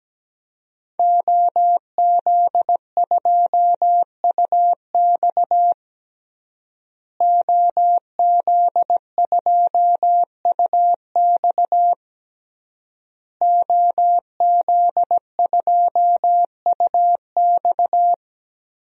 Morse code, heard as a keyed tone, about 17 wpm, 700 Hz. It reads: OZ2UX OZ2UX OZ2UX
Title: DXCC-100 Call Signs - 5 Characters - Repeated 3x - 17wpm